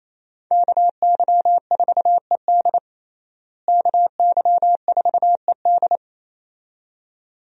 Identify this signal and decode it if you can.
Morse code, heard as a keyed tone, about 28 wpm, 700 Hz. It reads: KY4ED KY4ED
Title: USA Call Signs - 5 Characters - Repeated 2x - 28wpm